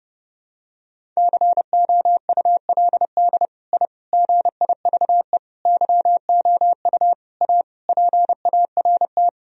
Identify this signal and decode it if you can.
Morse code, heard as a keyed tone, about 30 wpm, 700 Hz. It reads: COULD I GIVE YOU A PART
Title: Sentences from Top 1000 Words - Review Part 1 - 30wpm